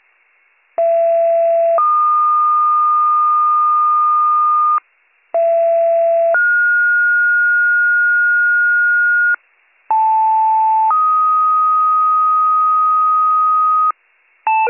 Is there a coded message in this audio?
Yes: two-tone page from 0.8 to 13.9 s, heard as a one-second tone, then a three-second tone, about 670 then 1220 Hz, about 670 then 1430 Hz, about 880 then 1250 Hz.